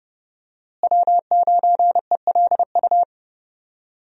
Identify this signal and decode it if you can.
Morse code, heard as a keyed tone, about 30 wpm, 700 Hz. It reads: W9ELU
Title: USA Call Signs - 5 Characters - Repeated 2x - 30wpm